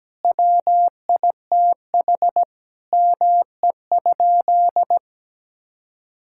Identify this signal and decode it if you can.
Morse code, heard as a keyed tone, about 17 wpm, 700 Hz. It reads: WITH ME?
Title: Sentences from Top 1600 Words 17wpm